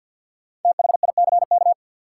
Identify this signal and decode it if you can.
Morse code, heard as a keyed tone, about 49 wpm, 700 Hz. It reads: THICK
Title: Sentences from Top 1600 Words - New Word Drill - 50wpm